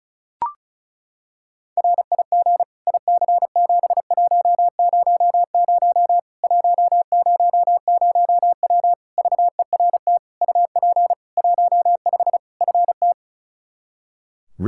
Morse code, heard as a keyed tone, about 34 wpm, 700 Hz. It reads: RIG IC7100 100W VERT UP 15 FT
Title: K1USN SST QSO Practice 35wpm